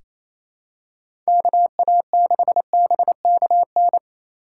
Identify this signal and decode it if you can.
Morse code, heard as a keyed tone, about 28 wpm, 700 Hz. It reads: KA6BKN
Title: USA Call Signs - 6 Characters - 3x Character Spacing - 28wpm